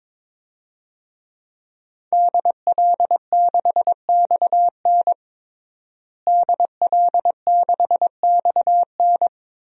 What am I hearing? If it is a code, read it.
Morse code, heard as a keyed tone, about 22 wpm, 700 Hz. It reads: DL6XN DL6XN
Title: DXCC-100 Call Signs - 5 Characters - Repeated 3x - 22wpm